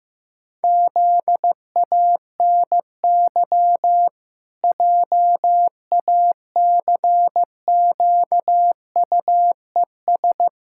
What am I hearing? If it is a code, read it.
Morse code, heard as a keyed tone, about 15 wpm, 700 Hz. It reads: ZANY JACQUES
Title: Pangram Challenge v2 - 15wpm